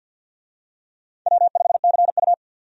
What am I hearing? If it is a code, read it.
Morse code, heard as a keyed tone, about 52 wpm, 700 Hz. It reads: W5CU
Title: USA Call Signs - 4 Characters - 50wpm